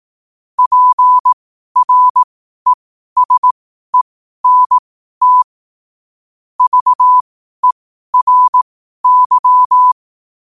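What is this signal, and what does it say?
Morse code, heard as a keyed tone, about 18 wpm, spaced as for 12 wpm, 1000 Hz. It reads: PRESENT VERY
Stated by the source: 12 wpm Morse code.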